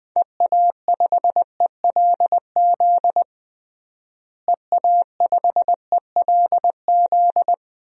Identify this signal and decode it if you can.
Morse code, heard as a keyed tone, about 20 wpm, 700 Hz. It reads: EA5ELZ EA5ELZ
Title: DXCC-100 Call Signs - 6 Characters - Repeated 3x - 20wpm